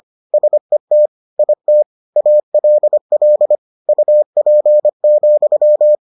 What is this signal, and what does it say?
Morse code, heard as a keyed tone, about 25 wpm, 600 Hz. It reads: SET IT ALL UP,